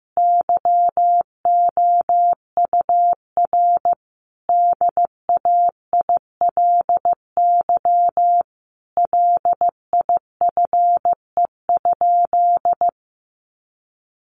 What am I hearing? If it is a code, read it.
Morse code, heard as a keyed tone, about 14 wpm, 700 Hz. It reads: YOUR DAILY LIFE?
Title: Sentences from Top 1100 Words 15wpm